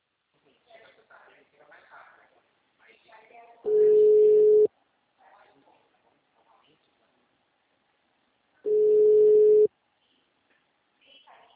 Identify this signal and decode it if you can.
Ringback tone (Europe, 425 Hz), with speech.
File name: out- -2022- - - .8208